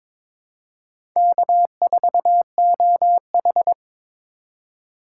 Morse code, heard as a keyed tone, about 22 wpm, 700 Hz. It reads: K4OH
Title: USA Call Signs - 4 Characters - 2x Character Spacing - 22wpm